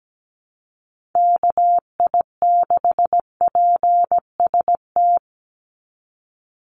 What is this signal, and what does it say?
Morse code, heard as a keyed tone, about 17 wpm, 700 Hz. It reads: KI6PST
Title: USA Call Signs 17wpm